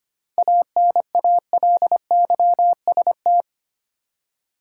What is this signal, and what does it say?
Morse code, heard as a keyed tone, about 25 wpm, 700 Hz. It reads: ANALYST
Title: Seven Letter Words - 25wpm